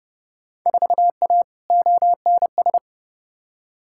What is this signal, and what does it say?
Morse code, heard as a keyed tone, about 32 wpm, 700 Hz. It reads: 4A ONS
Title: Field Day Exchanges - 30wpm